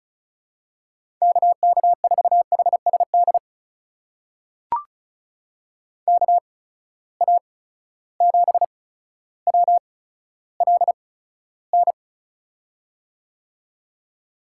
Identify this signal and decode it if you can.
Morse code, heard as a keyed tone, about 36 wpm, spaced as for 15 wpm, 700 Hz. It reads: KK4HSD KA7WLN
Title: USA Call Signs - 6 Characters - 8x Character Spacing - 35wpm